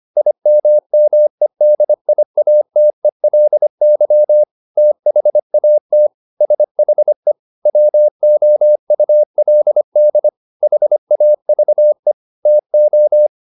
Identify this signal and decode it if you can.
Morse code, heard as a keyed tone, about 25 wpm, 600 Hz. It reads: IMMEDIATELY THAT SHE WOULD HAVE TO